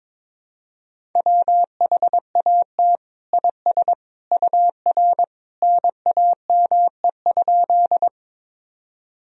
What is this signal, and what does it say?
Morse code, heard as a keyed tone, about 22 wpm, 700 Hz. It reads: WHAT IS UR NAME?
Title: Sentences from Top 200 Words and Abbreviations 22wpm